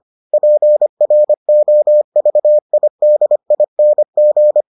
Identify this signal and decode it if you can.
Morse code, heard as a keyed tone, about 25 wpm, 600 Hz. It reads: PROVIDING